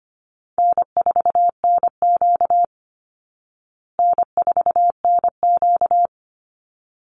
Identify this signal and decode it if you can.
Morse code, heard as a keyed tone, about 25 wpm, 700 Hz. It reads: N4NQ N4NQ
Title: Call Signs Repeated 3x - 25wpm